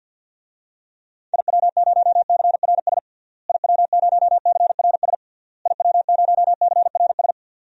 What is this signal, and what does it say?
Morse code, heard as a keyed tone, about 49 wpm, 700 Hz. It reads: IW0CRS IW0CRS IW0CRS
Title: DXCC-100 Call Signs - 6 Characters - Repeated 3x - 50wpm